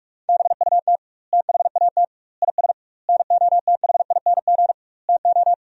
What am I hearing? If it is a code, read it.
Morse code, heard as a keyed tone, about 46 wpm, 700 Hz. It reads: BUT THAT IS NOTHING TO
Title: 2x Word-Spacing - Sentences from Top 500 Words - 45wpm